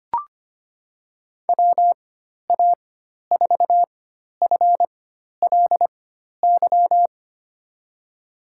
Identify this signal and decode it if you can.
Morse code, heard as a keyed tone, about 25 wpm, spaced as for 12 wpm, 700 Hz. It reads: WA4FLY
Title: USA Call Signs - 6 Characters - 4x Character Spacing - 25wpm